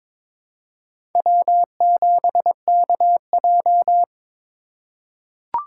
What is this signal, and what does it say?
Morse code, heard as a keyed tone, about 22 wpm, 700 Hz. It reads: W7KJ
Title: USA Call Signs - 2x Character Spacing - 22wpm